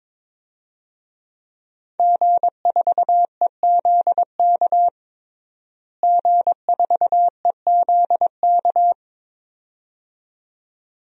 Morse code, heard as a keyed tone, about 22 wpm, 700 Hz. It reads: G4EZK G4EZK
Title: DXCC-100 Call Signs - 5 Characters - Repeated 2x - 22wpm